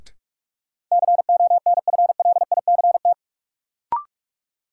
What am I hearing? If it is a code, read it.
Morse code, heard as a keyed tone, about 46 wpm, 700 Hz. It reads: CONFLICT
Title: Sentences from Top 1300 Words - New Word Drill - 45wpm